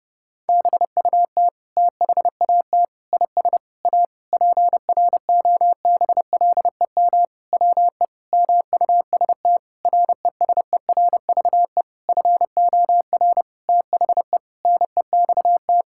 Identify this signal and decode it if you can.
Morse code, heard as a keyed tone, about 32 wpm, 700 Hz. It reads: BUT THAT IS A PROBLEM WE MUST RESERVE FOR THE NEXT
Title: Sentences from Top 1000 Words - Review Part 6 - 30wpm